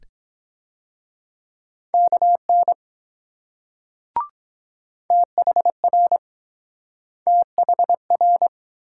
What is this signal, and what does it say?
Morse code, heard as a keyed tone, about 22 wpm, 700 Hz. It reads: KN THR THR